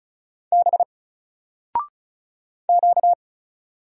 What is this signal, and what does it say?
Morse code, heard as a keyed tone, about 36 wpm, 700 Hz. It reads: B Q
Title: Single Letter-Number 35wpm